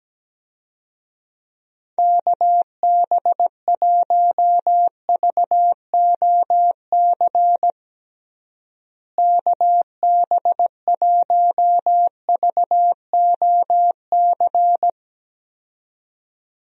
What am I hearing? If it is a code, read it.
Morse code, heard as a keyed tone, about 17 wpm, 700 Hz. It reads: KB1VOC KB1VOC
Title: USA Call Signs - 6 Characters - Repeated 2x - 17wpm